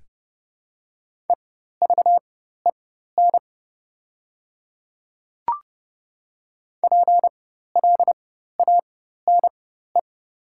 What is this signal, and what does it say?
Morse code, heard as a keyed tone, about 30 wpm, spaced as for 14 wpm, 690 Hz. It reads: EVEN PLANE
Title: 4x Character Spacing - Top 500 Words - 30wpm